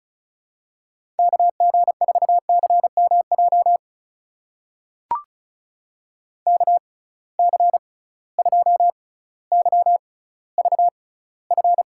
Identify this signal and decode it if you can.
Morse code, heard as a keyed tone, about 36 wpm, spaced as for 18 wpm, 700 Hz. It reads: KG4CMJ KC2YVF
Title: USA Call Signs - 6 Characters - 6x Character Spacing - 35wpm